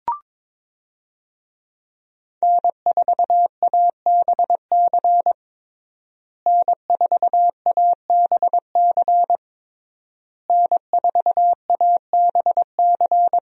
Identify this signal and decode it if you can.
Morse code, heard as a keyed tone, about 22 wpm, 700 Hz. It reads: N4ABC N4ABC N4ABC
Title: USA Call Signs - 5 Characters - Repeated 3x - 22wpm